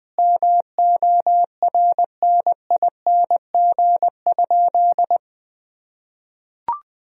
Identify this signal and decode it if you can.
Morse code, heard as a keyed tone, about 20 wpm, 700 Hz. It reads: MORNING?
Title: Sentences from Top 1000 Words - Review Part 2 - 20wpm